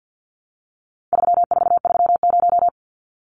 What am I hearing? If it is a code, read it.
Morse code, heard as a keyed tone, about 48 wpm, 700 Hz. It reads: 3430
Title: Numbers - 4-Digit 50wpm